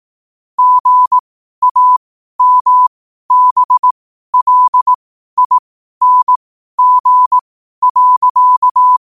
Morse code, heard as a keyed tone, about 18 wpm, spaced as for 12 wpm, 1000 Hz. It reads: GAMBLING.
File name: qotdfiles / 12wpm